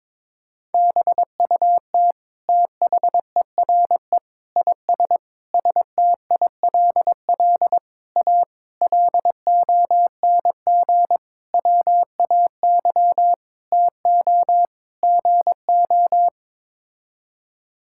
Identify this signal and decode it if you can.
Morse code, heard as a keyed tone, about 21 wpm, 700 Hz. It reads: BUT THERE IS STILL A LONG WAY TO GO
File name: Sets of 9 Words - 22wpm